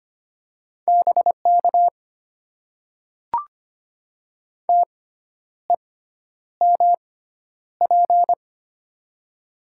Morse code, heard as a keyed tone, about 25 wpm, spaced as for 10 wpm, 690 Hz. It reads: BK TEMP
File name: Top 90 QSO Elements - 6x Character Spacing - 25wpm